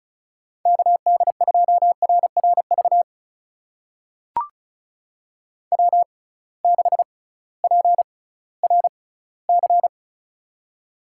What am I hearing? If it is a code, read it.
Morse code, heard as a keyed tone, about 36 wpm, spaced as for 19 wpm, 700 Hz. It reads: KD2RRV W6PRC
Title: USA Call Signs - 6x Character Spacing - 35wpm